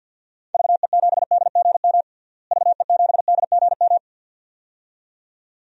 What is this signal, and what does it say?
Morse code, heard as a keyed tone, about 52 wpm, 700 Hz. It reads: VE7DGM VE7DGM
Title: DXCC-100 Call Signs - 6 Characters - Repeated 2x - 50wpm